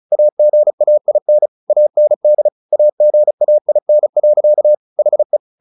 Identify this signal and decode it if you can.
Morse code, heard as a keyed tone, about 34 wpm, 600 Hz. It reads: AGAIN AND AGAIN. HE
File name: the old man and the sea 2 35wpm